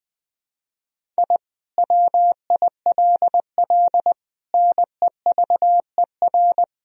Morse code, heard as a keyed tone, about 20 wpm, 700 Hz. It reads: I WILL NEVER
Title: Sentences from Top 1000 Words - Review Part 1 - 20wpm